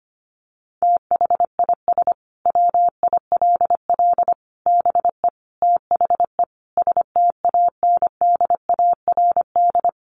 Morse code, heard as a keyed tone, about 25 wpm, 700 Hz. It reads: THIS WILL BE THE STANDARD
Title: Sentences from Top 1200 Words 25wpm